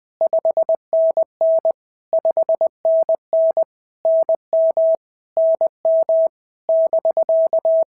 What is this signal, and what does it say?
Morse code, heard as a keyed tone, about 19 wpm, 650 Hz. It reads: 5NN 5NN NM NM <BK>
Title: POTA QSO Practice - 20wpm